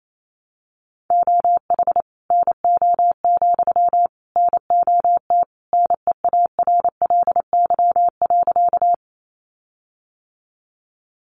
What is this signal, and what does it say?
Morse code, heard as a keyed tone, about 28 wpm, 700 Hz. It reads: OH NO, NOT NEARLY.
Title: Sentences from Top 1300 Words 28wpm